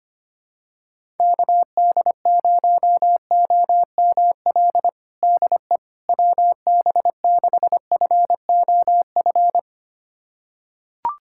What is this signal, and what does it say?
Morse code, heard as a keyed tone, about 25 wpm, 700 Hz. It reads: KD0OML DE WB6FOF
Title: USA Callsign DE Callsign 25wpm